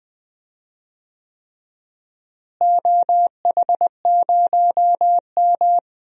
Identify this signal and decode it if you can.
Morse code, heard as a keyed tone, about 21 wpm, 700 Hz. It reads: OH0M
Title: DXCC-100 Call Signs - 4 Characters - Repeated 3x - 20wpm